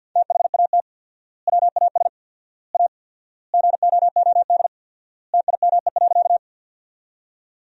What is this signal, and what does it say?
Morse code, heard as a keyed tone, about 48 wpm, 700 Hz. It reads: THAT WAS A GOOD TIME.
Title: 4x Word-Spacing - Sentences from Top 200 Words - 50wpm